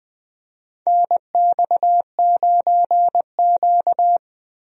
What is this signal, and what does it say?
Morse code, heard as a keyed tone, about 21 wpm, 700 Hz. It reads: NX9Q
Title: USA Call Signs - 4 Characters - 20wpm